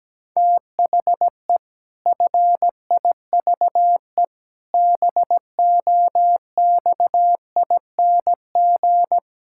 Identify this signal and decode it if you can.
Morse code, heard as a keyed tone, about 17 wpm, 700 Hz. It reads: THE FIVE BOXING
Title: Pangram Challenge v2 - 17wpm